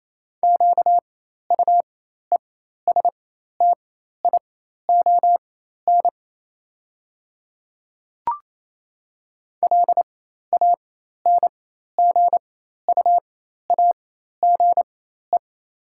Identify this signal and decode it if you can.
Morse code, heard as a keyed tone, about 28 wpm, spaced as for 13 wpm, 700 Hz. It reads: QUESTION LANGUAGE
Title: 4x Character Spacing - Top 500 Words - 28wpm